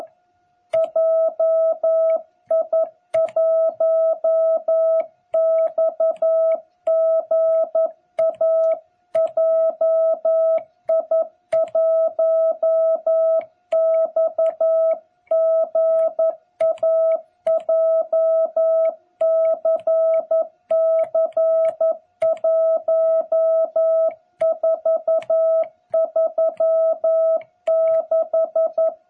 Morse code, heard as a keyed tone, about 11 wpm, 660 Hz. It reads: JI1XGA JI1XGA JCC1436
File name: jh8cbh / 2021cw